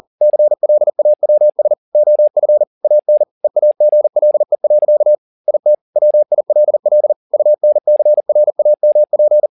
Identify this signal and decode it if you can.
Morse code, heard as a keyed tone, about 40 wpm, 600 Hz. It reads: CLAWS OF AN EAGLE. IT WILL UNCRAMP